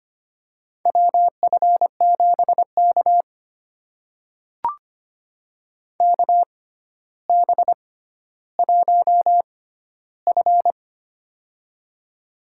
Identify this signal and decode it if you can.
Morse code, heard as a keyed tone, about 25 wpm, spaced as for 13 wpm, 700 Hz. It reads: WF7K KB1F
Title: USA Call Signs - 4 Characters - 6x Character Spacing - 25wpm